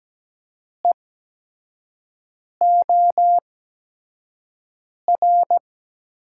Morse code, heard as a keyed tone, about 17 wpm, spaced as for 5 wpm, 700 Hz. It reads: EOR
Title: USA Call Signs - 8x Character Spacing - 17wpm